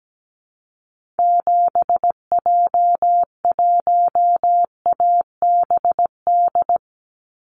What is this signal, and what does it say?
Morse code, heard as a keyed tone, about 17 wpm, 700 Hz. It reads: 7J1ABD
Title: Call Signs Repeated 3x - 17wpm